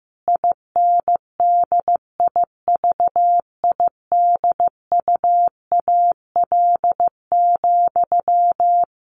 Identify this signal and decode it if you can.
Morse code, heard as a keyed tone, about 14 wpm, 700 Hz. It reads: INDIVIDUAL,
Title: Sentences from Top 1400 Words 15wpm